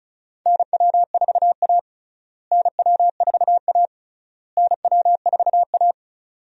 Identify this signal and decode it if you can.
Morse code, heard as a keyed tone, about 36 wpm, 700 Hz. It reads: NW4A NW4A NW4A
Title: USA Call Signs - 4 Characters - Repeated 3x - 35wpm